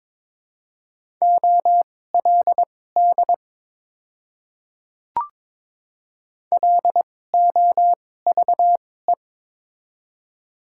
Morse code, heard as a keyed tone, about 22 wpm, spaced as for 16 wpm, 700 Hz. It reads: OLD LOVE
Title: 2x Character Spacing - Top 500 Words - 22wpm